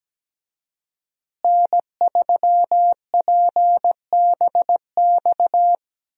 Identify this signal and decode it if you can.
Morse code, heard as a keyed tone, about 17 wpm, 700 Hz. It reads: N3PBX
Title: USA Call Signs - 5 Characters - Repeated 2x - 17wpm